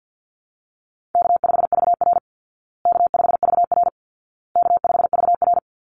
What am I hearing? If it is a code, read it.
Morse code, heard as a keyed tone, about 50 wpm, 700 Hz. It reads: K5VR K5VR K5VR
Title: Call Signs Repeated 2x - 50wpm